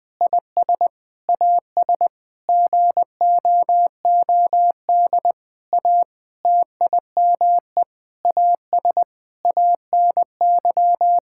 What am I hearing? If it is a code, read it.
Morse code, heard as a keyed tone, about 20 wpm, 700 Hz. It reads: IS AS GOOD A TIME AS ANY
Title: Sentences with 8-Word Phrases - Spoken Beginning - 20wpm